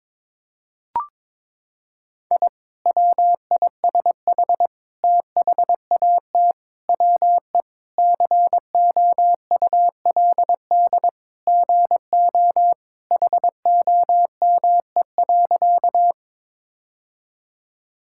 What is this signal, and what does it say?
Morse code, heard as a keyed tone, about 22 wpm, 700 Hz. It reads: I WISH THAT WE COULD GO HOME.
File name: practice / 22wpm / Sentences with 2-Word Phrases - 22wpm